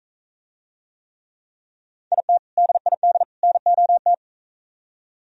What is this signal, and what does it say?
Morse code, heard as a keyed tone, about 43 wpm, 700 Hz. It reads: IT DID NOT